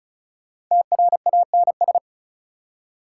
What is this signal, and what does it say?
Morse code, heard as a keyed tone, about 48 wpm, spaced as for 42 wpm, 700 Hz. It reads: TRANS
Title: Speed-Racing Prefixes and Suffixes 35wpm